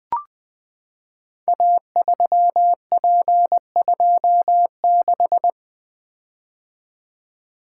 Morse code, heard as a keyed tone, about 20 wpm, 700 Hz. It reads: A3P26
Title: Code Groups - 5 Letter:Numbers - 20wpm